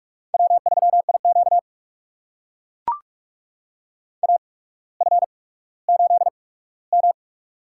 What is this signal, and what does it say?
Morse code, heard as a keyed tone, about 44 wpm, spaced as for 19 wpm, 700 Hz. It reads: W3IQ AF8M
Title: USA Call Signs - 8x Character Spacing - 45wpm